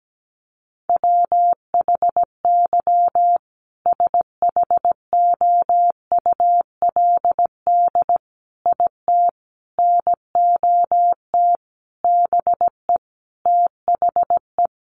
Morse code, heard as a keyed tone, about 17 wpm, 700 Hz. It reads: WHY SHOULD IT NOT BE THE
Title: Sentences from Top 1400 Words 17wpm